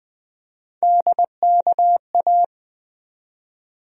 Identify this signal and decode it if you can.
Morse code, heard as a keyed tone, about 21 wpm, 700 Hz. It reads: DKA
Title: USA Call Sign Postfixes - 3 Letters 20wpm